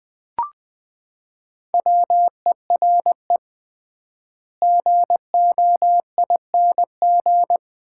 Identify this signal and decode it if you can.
Morse code, heard as a keyed tone, about 20 wpm, 700 Hz. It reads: WERE GOING
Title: 3x Word-Spacing - Sets of 3 Words - 20wpm